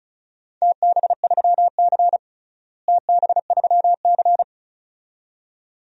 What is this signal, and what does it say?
Morse code, heard as a keyed tone, about 36 wpm, 700 Hz. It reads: TB3C TB3C
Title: DXCC-100 Call Signs - 4 Characters - Repeated 2x - 35wpm